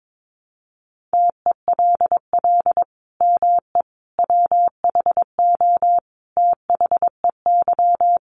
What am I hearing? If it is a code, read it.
Morse code, heard as a keyed tone, about 22 wpm, 700 Hz. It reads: TELL ME WHO THEY
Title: Sentences from Top 700 Words 22wpm